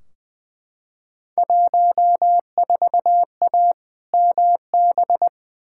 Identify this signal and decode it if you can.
Morse code, heard as a keyed tone, about 20 wpm, 700 Hz. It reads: 14A MB
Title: Field Day Exchanges Encore - 20wpm